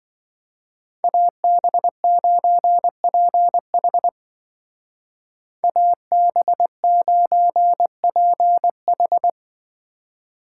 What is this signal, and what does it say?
Morse code, heard as a keyed tone, about 24 wpm, 700 Hz. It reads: AB9PH AB9PH
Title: Speed Racing - USA 2x2 Call Signs 20wpm